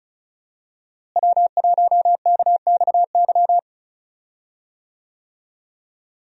Morse code, heard as a keyed tone, about 36 wpm, 700 Hz. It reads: W1KXY
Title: USA Call Signs - 5 Characters - Repeated 3x - 35wpm